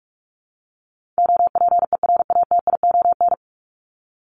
Morse code, heard as a keyed tone, about 44 wpm, 700 Hz. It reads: OPERATION
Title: Top 1501-1750 Words 45wpm